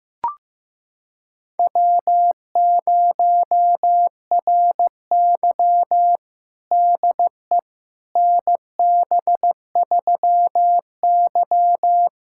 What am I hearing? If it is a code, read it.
Morse code, heard as a keyed tone, about 16 wpm, 700 Hz. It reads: W0RY DE NB3Y
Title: USA Callsign DE Callsign - 4 Characters - 15wpm